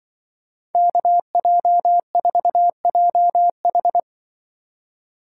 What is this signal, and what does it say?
Morse code, heard as a keyed tone, about 24 wpm, 700 Hz. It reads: KJ4JH
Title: Speed Racing - USA 2x2 Call Signs 20wpm